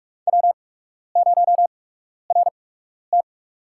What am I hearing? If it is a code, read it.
Morse code, heard as a keyed tone, about 46 wpm, spaced as for 12 wpm, 690 Hz. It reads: W0RT
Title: USA Call Signs - 4 Characters - 8x Character Spacing - 45wpm